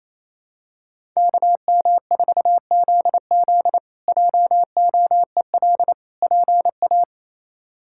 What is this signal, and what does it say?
Morse code, heard as a keyed tone, about 28 wpm, 700 Hz. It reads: KM4ZZ JOEL PA
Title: K1USN SST QSO Practice - 28wpm